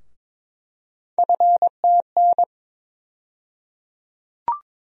Morse code, heard as a keyed tone, about 22 wpm, 700 Hz. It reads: FTN
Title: USA Call Sign Postfixes - 3 Letters 22wpm